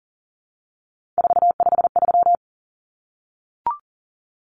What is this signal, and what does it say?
Morse code, heard as a keyed tone, about 40 wpm, 700 Hz. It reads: 453
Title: Common Cut Numbers - 3-Digit 40wpm